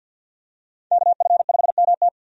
Morse code, heard as a keyed tone, about 53 wpm, 700 Hz. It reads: KF5MT
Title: USA Call Signs - 5 Characters - 50wpm